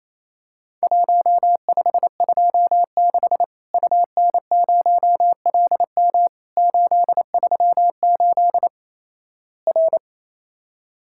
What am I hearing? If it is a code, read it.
Morse code, heard as a keyed tone, about 28 wpm, 700 Hz. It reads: 1526 UN0LM 838 R
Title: Worked All Europe DX Contest - QTC Exchange - 28wpm